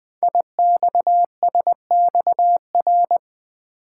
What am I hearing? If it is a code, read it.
Morse code, heard as a keyed tone, about 20 wpm, 700 Hz. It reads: IXSXR
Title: Code Groups - 5 Letter:Numbers - 20wpm